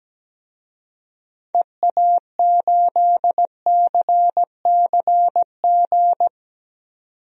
Morse code, heard as a keyed tone, about 17 wpm, 700 Hz. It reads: EA8CCG
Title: DXCC-100 Call Signs - 6 Characters - 17wpm